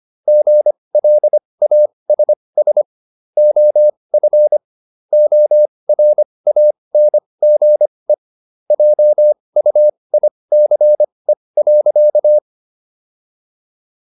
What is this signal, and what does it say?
Morse code, heard as a keyed tone, about 25 wpm, spaced as for 20 wpm, 600 Hz. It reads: GLASS OF ORANGE JUICE.